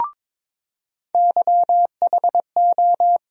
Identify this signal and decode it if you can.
Morse code, heard as a keyed tone, about 22 wpm, 700 Hz. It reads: YHO
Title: USA Call Sign Postfixes - 3 Letters 22wpm